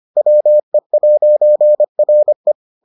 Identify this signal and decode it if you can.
Morse code, heard as a keyed tone, about 25 wpm, 600 Hz. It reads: WE'RE